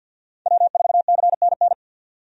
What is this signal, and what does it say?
Morse code, heard as a keyed tone, about 51 wpm, 700 Hz. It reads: W4CNN
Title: USA Call Signs - 5 Characters - 50wpm